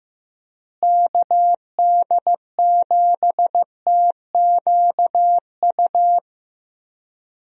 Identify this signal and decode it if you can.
Morse code, heard as a keyed tone, about 15 wpm, 700 Hz. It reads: KD7TQU
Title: USA Call Signs - 6 Characters - Repeated 3x - 15wpm